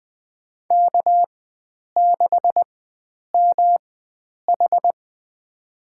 Morse code, heard as a keyed tone, about 20 wpm, spaced as for 9 wpm, 700 Hz. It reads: K6MH
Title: USA Call Signs - 4 Characters - 4x Character Spacing - 20wpm